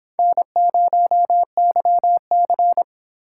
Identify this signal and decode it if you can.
Morse code, heard as a keyed tone, about 26 wpm, 700 Hz. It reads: N0YC